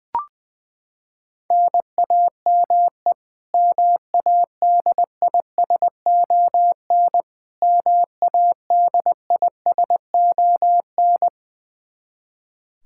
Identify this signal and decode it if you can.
Morse code, heard as a keyed tone, about 20 wpm, 700 Hz. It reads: NAME MADISON MADISON
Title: K1USN SST QSO Practice 20wpm